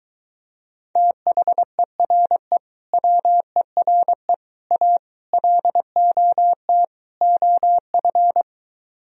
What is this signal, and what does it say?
Morse code, heard as a keyed tone, about 23 wpm, 700 Hz. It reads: THERE WERE A LOT OF